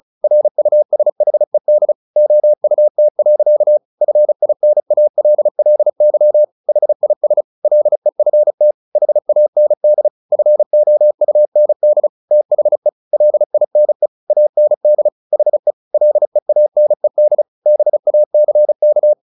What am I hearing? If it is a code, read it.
Morse code, heard as a keyed tone, about 34 wpm, 600 Hz. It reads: RUSHED OUT. FINALLY HIS LEFT HAND FOUND THE LINE AND HE LEANED BACK
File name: the old man and the sea 2 35wpm